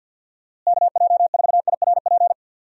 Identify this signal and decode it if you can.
Morse code, heard as a keyed tone, about 51 wpm, 700 Hz. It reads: KJ4IRP
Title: USA Call Signs - 6 Characters - Repeated 3x - 50wpm